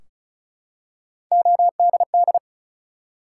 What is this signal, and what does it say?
Morse code, heard as a keyed tone, about 36 wpm, 700 Hz. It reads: ODD
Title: Three Letter Words - 35wpm